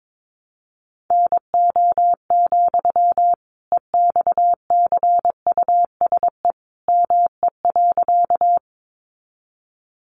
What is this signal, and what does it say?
Morse code, heard as a keyed tone, about 22 wpm, 700 Hz. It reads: NO, EXCUSE ME.
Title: Sentences from Top 700 Words 22wpm